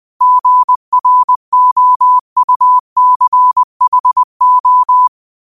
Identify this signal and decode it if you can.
Morse code, heard as a keyed tone, about 20 wpm, 1000 Hz. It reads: GROUCHO
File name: qotdfiles / 20wpm